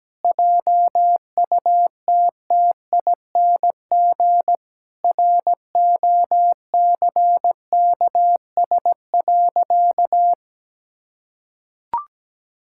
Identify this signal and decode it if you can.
Morse code, heard as a keyed tone, about 17 wpm, 700 Hz. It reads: JUTTING ROCKS.
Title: Pangram Challenge v2 - 17wpm